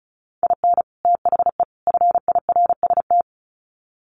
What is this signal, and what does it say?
Morse code, heard as a keyed tone, about 34 wpm, 700 Hz. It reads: IN THE FIRST
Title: Sets of 3 Words 35wpm